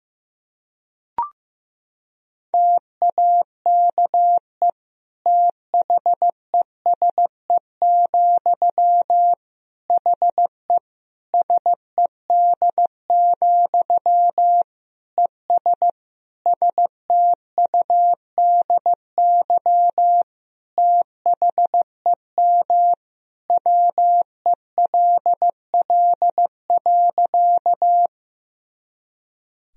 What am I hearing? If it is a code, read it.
Morse code, heard as a keyed tone, about 16 wpm, 700 Hz. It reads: TAKE THESE, HE SED, ES STUDY THEM WELL.
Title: Sentences from Top 300 Words and Abbreviations 15wpm